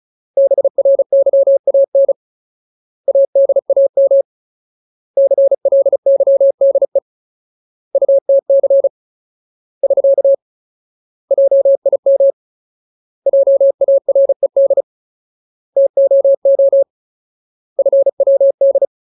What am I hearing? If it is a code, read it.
Morse code, heard as a keyed tone, about 34 wpm, 550 Hz. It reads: BRYAN ADAM CLYDE UTC <SK> JIM JARED TOO FWD
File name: Ultimate Word List / 35WPM Ultimate